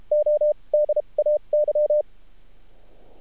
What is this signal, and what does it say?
Morse code, heard as a keyed tone, about 33 wpm, spaced as for 26 wpm, 600 Hz. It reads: ODAY
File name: archived / CRSnet